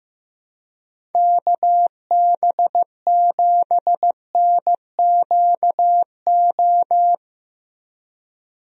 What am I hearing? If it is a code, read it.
Morse code, heard as a keyed tone, about 15 wpm, 700 Hz. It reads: KB7NQO
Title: USA Call Signs - 6 Characters - Repeated 3x - 15wpm